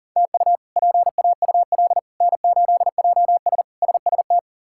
Morse code, heard as a keyed tone, about 40 wpm, 700 Hz. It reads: TU PAUL N8JS SST
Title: K1USN SST QSO Practice - 40wpm